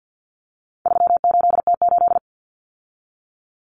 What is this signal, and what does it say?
Morse code, heard as a keyed tone, about 50 wpm, 700 Hz. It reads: 38T8
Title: Common Cut Numbers - 4-Digit 50wpm